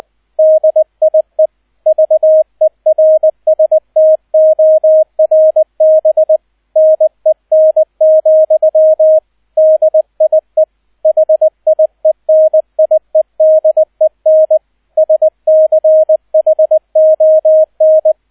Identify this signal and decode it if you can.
Morse code, heard as a keyed tone, about 20 wpm, 630 Hz, with radio noise. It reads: DIE VERSTORBNEN, DIE HIENIEDEN SCHON
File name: 002 Zweiter Streich 20wpm